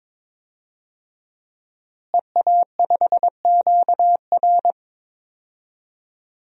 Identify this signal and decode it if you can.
Morse code, heard as a keyed tone, about 22 wpm, 700 Hz. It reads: EA5QR
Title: DXCC-100 Call Signs - 5 Characters - Repeated 2x - 22wpm